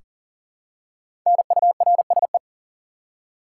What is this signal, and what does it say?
Morse code, heard as a keyed tone, about 40 wpm, 700 Hz. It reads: NURSE
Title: Five Letter Words - 40wpm